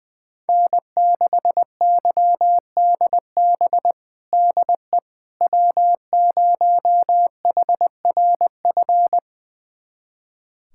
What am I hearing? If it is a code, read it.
Morse code, heard as a keyed tone, about 20 wpm, 700 Hz. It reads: N6YDB DE W0HRF
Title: USA Callsign DE Callsign - 5 Characters - 20wpm